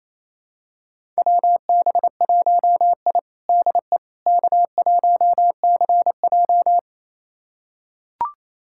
Morse code, heard as a keyed tone, about 28 wpm, 700 Hz. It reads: WB1I DE K1CJ
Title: USA Callsign DE Callsign - 4 Characters - 28wpm